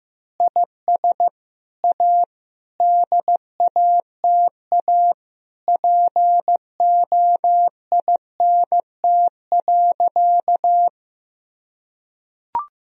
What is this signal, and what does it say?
Morse code, heard as a keyed tone, about 16 wpm, 700 Hz. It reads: IS A DATA POINT.
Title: Sentences from Top 1000 Words - Review Part 3 - 15wpm